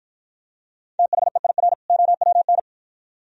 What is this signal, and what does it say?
Morse code, heard as a keyed tone, about 53 wpm, 700 Hz. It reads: THEIR OWN